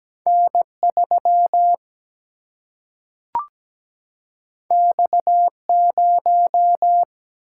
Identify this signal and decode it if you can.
Morse code, heard as a keyed tone, about 17 wpm, 700 Hz. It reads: N3 X0
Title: Single-Letter Single-Number 17wpm